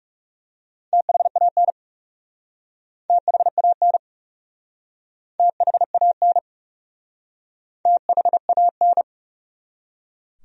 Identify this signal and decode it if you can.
Morse code, heard as a keyed tone, about 40 wpm, 700 Hz. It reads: THAN THAN THAN THAN